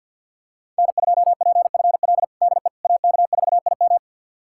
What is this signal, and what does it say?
Morse code, heard as a keyed tone, about 51 wpm, 700 Hz. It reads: N2PFL DE AK4IM
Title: USA Callsign DE Callsign - 5 Characters - 50wpm